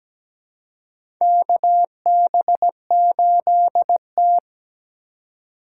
Morse code, heard as a keyed tone, about 17 wpm, 700 Hz. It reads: KB8T
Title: USA Call Signs - 4 Characters - 2x Character Spacing - 17wpm